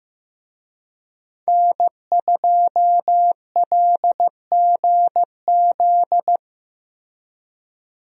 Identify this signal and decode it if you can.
Morse code, heard as a keyed tone, about 15 wpm, 700 Hz. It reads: N2LGZ
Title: USA Call Signs - 5 Characters - Repeated 3x - 15wpm